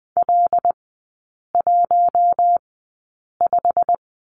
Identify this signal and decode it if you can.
Morse code, heard as a keyed tone, about 20 wpm, spaced as for 8 wpm, 700 Hz. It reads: L15
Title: 107 Lesson 18e - Letter m - Three Characters 20wpm